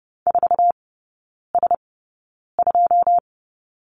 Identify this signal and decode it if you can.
Morse code, heard as a keyed tone, about 30 wpm, spaced as for 9 wpm, 690 Hz. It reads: 4S2
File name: Lesson 16e - Letter c - Three Characters 30wpm